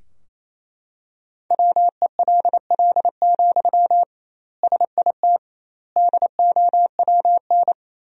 Morse code, heard as a keyed tone, about 28 wpm, 700 Hz. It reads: WELL, SIT DOWN
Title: 2x Word-Spacing - Sentences from Top 500 Words - 28wpm